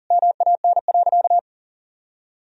Morse code, heard as a keyed tone, about 40 wpm, 700 Hz. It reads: MAN.
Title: Sentences from Top 1000 Words - Review Part 1 - 40wpm